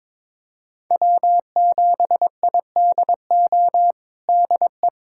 Morse code, heard as a keyed tone, about 22 wpm, 700 Hz. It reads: W7IDO DE
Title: USA Callsign DE Callsign - 5 Characters - 22wpm